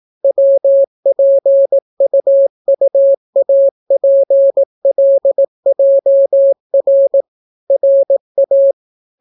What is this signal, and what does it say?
Morse code, heard as a keyed tone, about 17 wpm, 550 Hz. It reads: WPUUAPLJR RA